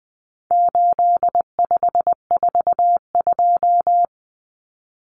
Morse code, heard as a keyed tone, about 20 wpm, 700 Hz. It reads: 8542
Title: Common Cut Numbers - 4-Digit 20wpm